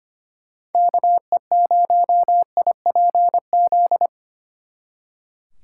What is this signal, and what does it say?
Morse code, heard as a keyed tone, about 30 wpm, 700 Hz. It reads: KE0IPZ
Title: Speed Racing - USA 2x3 Call Signs 25wpm